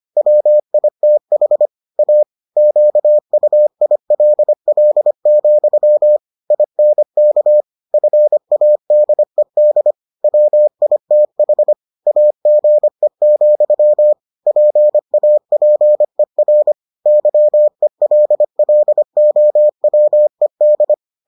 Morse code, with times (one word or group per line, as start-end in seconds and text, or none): 0.2-1.7 WITH
2.0-2.2 A
2.6-6.2 QUILL,
6.5-7.6 INK
7.9-9.9 FADED
10.2-11.7 WITH
12.1-14.1 AGE,
14.5-16.7 PAPER
17.1-20.9 YELLOWED